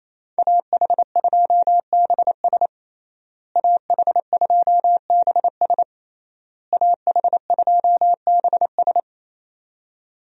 Morse code, heard as a keyed tone, about 27 wpm, 700 Hz. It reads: AH2BS AH2BS AH2BS
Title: USA Call Signs - 5 Characters - Repeated 3x - 28wpm